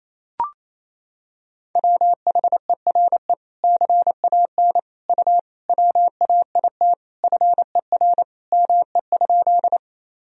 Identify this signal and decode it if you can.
Morse code, heard as a keyed tone, about 28 wpm, 700 Hz. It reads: WHERE CAN U WAIT FER ME?
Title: Sentences from Top 500 Words and Abbreviations 28wpm